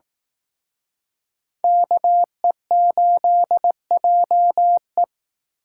Morse code, heard as a keyed tone, about 18 wpm, 700 Hz. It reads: KE8JE
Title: Speed Racing - USA 2x2 Call Signs 15wpm